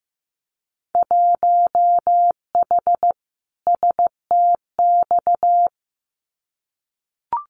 Morse code, heard as a keyed tone, about 15 wpm, 700 Hz. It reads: 1H STX
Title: Winter Field Day Exchanges Encore 15wpm